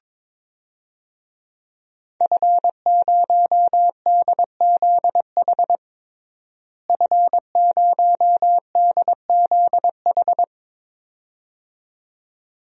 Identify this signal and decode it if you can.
Morse code, heard as a keyed tone, about 23 wpm, 700 Hz. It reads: F0DZH F0DZH
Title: DXCC-100 Call Signs - 5 Characters - Repeated 2x - 22wpm